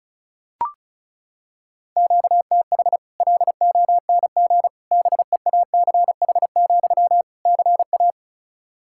Morse code, heard as a keyed tone, about 36 wpm, 700 Hz. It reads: QTH LONG BEACH, CA
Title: Practice QSO - Exchange 1 - Beginning - 35wpm